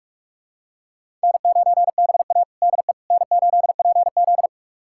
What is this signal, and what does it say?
Morse code, heard as a keyed tone, about 46 wpm, 700 Hz. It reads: N9BA DE N8PZ
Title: USA Callsign DE Callsign - 4 Characters - 45wpm